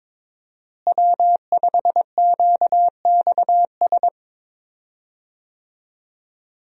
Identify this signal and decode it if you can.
Morse code, heard as a keyed tone, about 22 wpm, 700 Hz. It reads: W5QXS
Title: USA Call Signs - 5 Characters - Repeated 3x - 22wpm